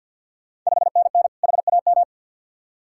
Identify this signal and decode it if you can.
Morse code, heard as a keyed tone, about 50 wpm, 690 Hz. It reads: 5NN HAM
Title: Ontario QSO Party Exchange 50wpm